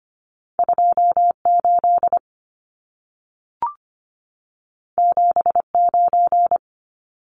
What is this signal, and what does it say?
Morse code, heard as a keyed tone, about 25 wpm, 700 Hz. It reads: 28 79
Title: Numbers - 2-Digit 25wpm